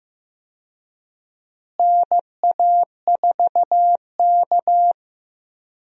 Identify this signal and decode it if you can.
Morse code, heard as a keyed tone, about 15 wpm, 700 Hz. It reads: NA4K
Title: USA Call Signs - 4 Characters - Repeated 2x - 15wpm